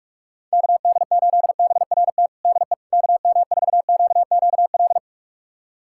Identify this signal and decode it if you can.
Morse code, heard as a keyed tone, about 41 wpm, 700 Hz. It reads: KD8BRT DE KM4QQL
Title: USA Callsign DE Callsign 45wpm